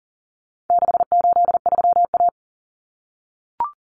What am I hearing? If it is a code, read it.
Morse code, heard as a keyed tone, about 40 wpm, 700 Hz. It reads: 683A
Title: Common Cut Numbers - 4-Digit 40wpm